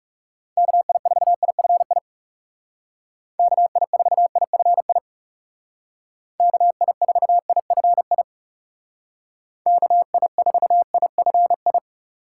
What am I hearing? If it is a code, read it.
Morse code, heard as a keyed tone, about 36 wpm, 700 Hz. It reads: KI4IFI KI4IFI KI4IFI KI4IFI
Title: Speed Racing - USA 2x3 Call Signs 30wpm